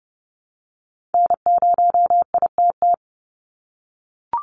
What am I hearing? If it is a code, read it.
Morse code, heard as a keyed tone, about 30 wpm, 700 Hz. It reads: N0ITT
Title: USA Call Signs 30wpm